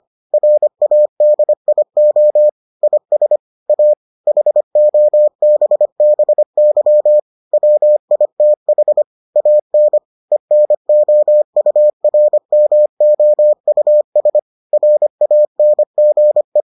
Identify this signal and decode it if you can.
Morse code, heard as a keyed tone, about 25 wpm, 600 Hz. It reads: RADIO IS A HOBBY WITH AN ENOURMOUS RANGE